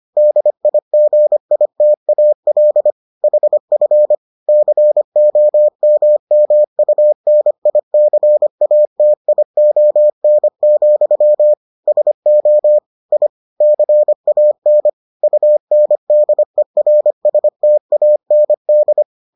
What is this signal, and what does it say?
Morse code, heard as a keyed tone, about 25 wpm, 600 Hz. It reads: DIGITAL HF COMMUNICATION, SO I CAN UNDERSTAND